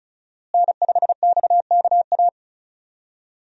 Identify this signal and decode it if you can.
Morse code, heard as a keyed tone, about 35 wpm, 700 Hz. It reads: N5XKA
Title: USA Call Signs - 5 Characters - Repeated 2x - 35wpm